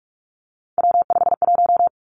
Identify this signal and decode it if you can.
Morse code, heard as a keyed tone, about 46 wpm, 700 Hz. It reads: W51
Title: Three Challenging Letter-Number Combinations 45wpm